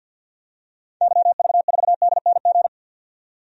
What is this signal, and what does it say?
Morse code, heard as a keyed tone, about 47 wpm, 700 Hz. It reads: YV4DNG
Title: DXCC-100 Call Signs - 6 Characters - 50wpm